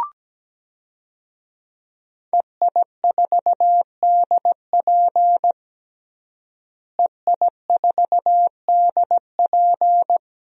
Morse code, heard as a keyed tone, about 17 wpm, 700 Hz. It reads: EI4DP EI4DP
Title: DXCC-100 Call Signs - 5 Characters - Repeated 2x - 17wpm